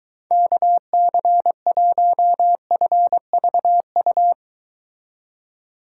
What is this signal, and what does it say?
Morse code, heard as a keyed tone, about 23 wpm, 700 Hz. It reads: KC1FVU